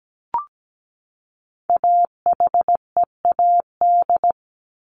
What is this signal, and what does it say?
Morse code, heard as a keyed tone, about 17 wpm, 700 Hz. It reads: AHEAD
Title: Top 1001-1250 Words 17wpm